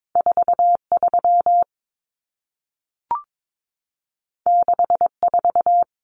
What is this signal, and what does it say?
Morse code, heard as a keyed tone, about 22 wpm, 700 Hz. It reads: 43 64
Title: Numbers - 2-Digit 22wpm